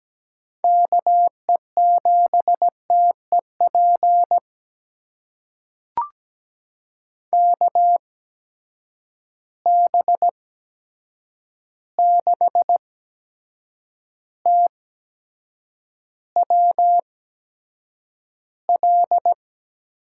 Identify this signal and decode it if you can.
Morse code, heard as a keyed tone, about 17 wpm, spaced as for 7 wpm, 700 Hz. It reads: KE7TEP KB6TWL
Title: USA Call Signs - 6 Characters - 8x Character Spacing - 17wpm